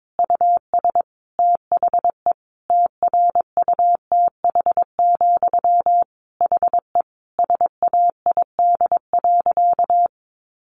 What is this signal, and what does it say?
Morse code, heard as a keyed tone, about 22 wpm, 700 Hz. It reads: US THE TRUTH, HE SAID.
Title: Sentences from Top 800 Words 22wpm